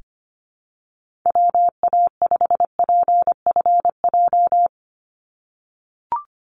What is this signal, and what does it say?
Morse code, heard as a keyed tone, about 25 wpm, 700 Hz. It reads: WA5PFJ
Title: Call Signs - 25wpm